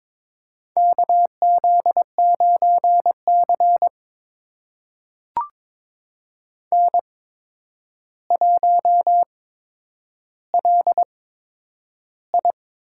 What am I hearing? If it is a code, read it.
Morse code, heard as a keyed tone, about 22 wpm, spaced as for 9 wpm, 700 Hz. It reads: KZ9C N1LI
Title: USA Call Signs - 4 Characters - 8x Character Spacing - 22wpm